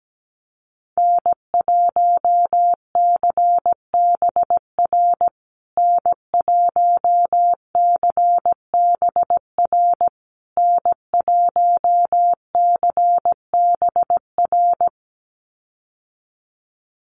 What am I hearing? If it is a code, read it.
Morse code, heard as a keyed tone, about 17 wpm, 700 Hz. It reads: N1CBR N1CBR N1CBR
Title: USA Call Signs Repeated 3x 17wpm